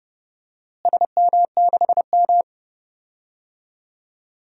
Morse code, heard as a keyed tone, about 30 wpm, 700 Hz. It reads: SM6M
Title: DXCC-100 Call Signs - 4 Characters - Repeated 2x - 30wpm